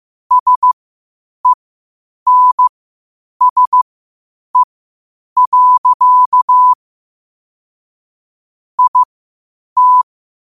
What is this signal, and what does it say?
Morse code, heard as a keyed tone, about 15 wpm, spaced as for 8 wpm, 1000 Hz. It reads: SENSE. IT
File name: qotdfiles / 8wpm